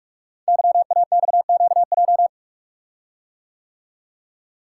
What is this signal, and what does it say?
Morse code, heard as a keyed tone, about 44 wpm, 690 Hz. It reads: YAXQJ
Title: Code Groups - 5 Letters - 45wpm